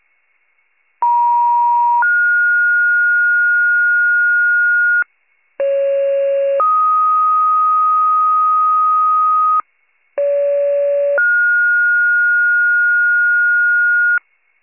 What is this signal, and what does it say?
Two-tone page, heard as a one-second tone, then a three-second tone, about 950 then 1430 Hz, about 550 then 1250 Hz, about 570 then 1430 Hz, with noise. It is heard from 1.0 to 14.2 s.
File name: wcalert